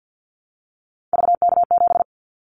Morse code, heard as a keyed tone, about 50 wpm, 700 Hz. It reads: VK7
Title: Callsign Prefixes for Top 100 DXCC - 50wpm